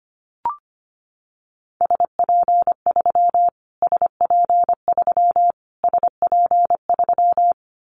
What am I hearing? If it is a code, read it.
Morse code, heard as a keyed tone, about 25 wpm, 700 Hz. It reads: SP3 SP3 SP3
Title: Callsign Prefixes for Top 25 DXCC - Repeated 3x - 25wpm